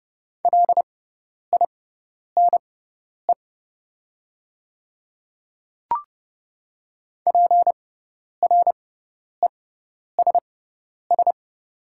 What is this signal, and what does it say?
Morse code, heard as a keyed tone, about 30 wpm, spaced as for 10 wpm, 700 Hz. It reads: LINE PRESS
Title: 6x Character Spacing - Top 300 Words - 30wpm